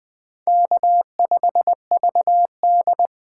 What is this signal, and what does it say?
Morse code, heard as a keyed tone, about 20 wpm, 700 Hz. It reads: K5VD
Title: USA Call Signs - 4 Characters - 2x Character Spacing - 20wpm